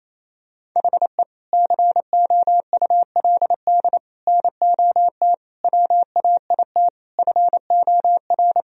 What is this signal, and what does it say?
Morse code, heard as a keyed tone, about 28 wpm, 700 Hz. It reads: HE COULD NOT WAIT FOR
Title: Sentences from Top 1000 Words - Review Part 1 - 28wpm